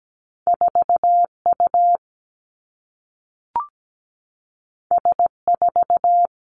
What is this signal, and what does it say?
Morse code, heard as a keyed tone, about 17 wpm, 700 Hz. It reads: 4U S4
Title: Two Challenging Letter-Number Combinations 17wpm